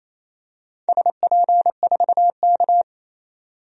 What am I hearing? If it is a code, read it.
Morse code, heard as a keyed tone, about 28 wpm, 700 Hz. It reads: SP4K
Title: DXCC-100 Call Signs - 4 Characters - 28wpm